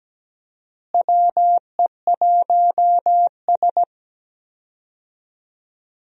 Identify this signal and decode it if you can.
Morse code, heard as a keyed tone, about 17 wpm, 700 Hz. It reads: WE1S
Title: USA Call Signs - 4 Characters - Repeated 2x - 17wpm